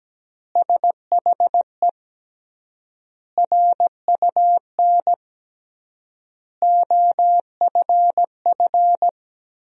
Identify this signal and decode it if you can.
Morse code, heard as a keyed tone, about 17 wpm, 700 Hz. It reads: SHE RUN OFF
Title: 3x Word-Spacing - Sentences from Top 300 Words - 17wpm